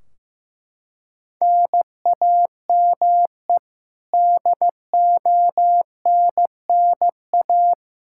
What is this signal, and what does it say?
Morse code, heard as a keyed tone, about 16 wpm, 700 Hz. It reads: NAME DONNA
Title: Practice QSO - Exchange 1 - Beginning - 15wpm